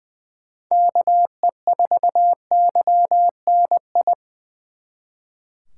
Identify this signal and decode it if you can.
Morse code, heard as a keyed tone, about 24 wpm, 700 Hz. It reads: KE4YNI
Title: Speed Racing - USA 2x3 Call Signs 20wpm